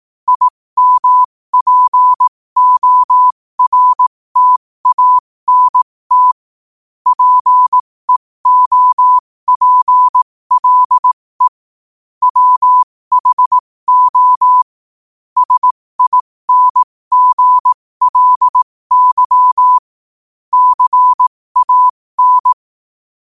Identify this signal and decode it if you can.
Morse code, heard as a keyed tone, about 18 wpm, spaced as for 15 wpm, 1000 Hz. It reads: IMPORTANT PEOPLE WHO SINGLY CAN
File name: qotdfiles / 15wpm